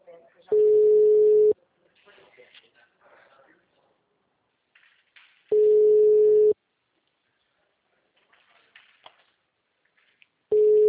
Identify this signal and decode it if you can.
Ringback tone (Europe, 425 Hz), with speech.